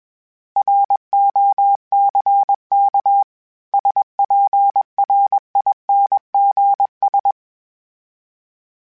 Morse code, heard as a keyed tone, about 21 wpm, 800 Hz. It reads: ROCK SPRINGS
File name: fromSMs / FromSMs mp3s / Cities@20